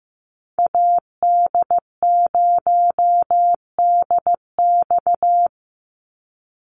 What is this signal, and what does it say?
Morse code, heard as a keyed tone, about 15 wpm, 700 Hz. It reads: AD0DX
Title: Call Signs Repeated 2x - 15wpm